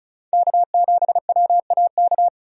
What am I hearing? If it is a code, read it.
Morse code, heard as a keyed tone, about 36 wpm, 700 Hz. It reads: K7WAK
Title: USA Call Signs - 5 Characters - 35wpm